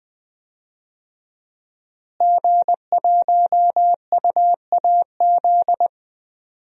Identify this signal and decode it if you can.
Morse code, heard as a keyed tone, about 20 wpm, 700 Hz. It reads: G1UAZ
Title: DXCC-100 Call Signs - 5 Characters - Repeated 2x - 20wpm